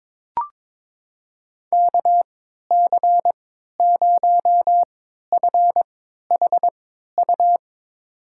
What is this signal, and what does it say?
Morse code, heard as a keyed tone, about 22 wpm, spaced as for 13 wpm, 700 Hz. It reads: KC0FHU
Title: USA Call Signs - 6 Characters - 3x Character Spacing - 22wpm